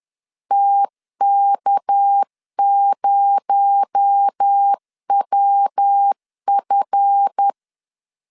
Morse code, heard as a keyed tone, about 10 wpm, 800 Hz. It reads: TK0WF
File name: LEZIONI / 10 SIGNS / 10D COMA 10wpm 3013